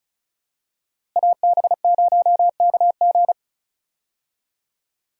Morse code, heard as a keyed tone, about 36 wpm, 700 Hz. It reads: AB0KG
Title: USA Call Signs - 5 Characters - Repeated 3x - 35wpm